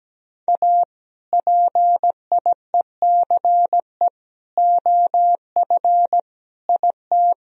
Morse code, heard as a keyed tone, about 17 wpm, 700 Hz. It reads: A PIECE OF IT
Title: Sentences from Top 1000 Words - Review Part 2 - 17wpm